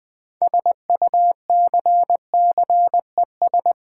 Morse code, heard as a keyed tone, about 20 wpm, 700 Hz. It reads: SUCCES
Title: Build A Word - 6 Letters At A Time - 12-Letter Words - 20wpm